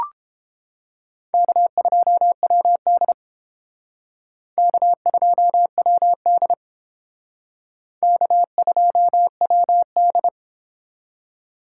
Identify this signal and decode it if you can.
Morse code, heard as a keyed tone, about 29 wpm, 700 Hz. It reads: K2WD K2WD K2WD